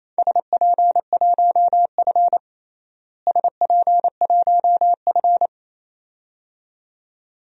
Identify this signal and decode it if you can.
Morse code, heard as a keyed tone, about 28 wpm, 700 Hz. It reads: SP1F SP1F
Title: DXCC-100 Call Signs - 4 Characters - Repeated 2x - 28wpm